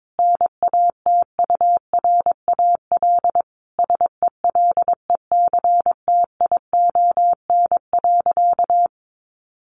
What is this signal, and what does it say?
Morse code, heard as a keyed tone, about 22 wpm, 700 Hz. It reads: NATURAL SELECTION.
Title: Sentences from Top 1400 Words 22wpm